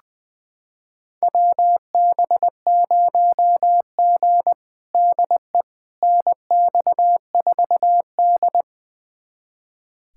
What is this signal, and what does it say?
Morse code, heard as a keyed tone, about 20 wpm, 700 Hz. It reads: WB0G DE NX4D
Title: USA Callsign DE Callsign - 4 Characters - 20wpm